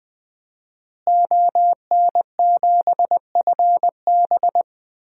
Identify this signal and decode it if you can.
Morse code, heard as a keyed tone, about 20 wpm, 700 Hz. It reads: ON7FB
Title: DXCC-100 Call Signs - 5 Characters - Repeated 3x - 20wpm